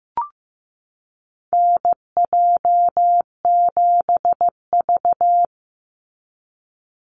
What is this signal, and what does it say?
Morse code, heard as a keyed tone, about 15 wpm, 700 Hz. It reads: NJ7V
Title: NA SOTA Chaser Callsigns 15wpm